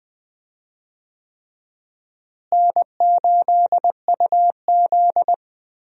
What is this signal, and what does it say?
Morse code, heard as a keyed tone, about 20 wpm, 700 Hz. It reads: N8UZ
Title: USA Call Signs - 4 Characters - Repeated 3x - 20wpm